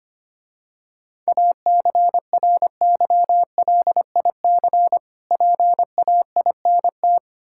Morse code, heard as a keyed tone, about 25 wpm, 700 Hz. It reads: ACRYLIC PAINT
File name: practice / 25wpm / Difficult Words - Speed-Racing with Hints - 25wpm